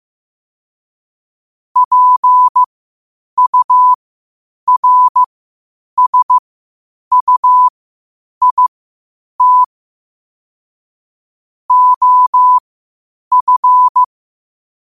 Morse code, heard as a keyed tone, about 15 wpm, spaced as for 8 wpm, 1000 Hz. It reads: PURSUIT OF
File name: qotdfiles / 8wpm